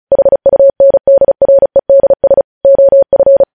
Morse code, heard as a keyed tone, about 36 wpm, 550 Hz. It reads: HUNDREDS OF
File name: podcast / News headlines-1x-35x35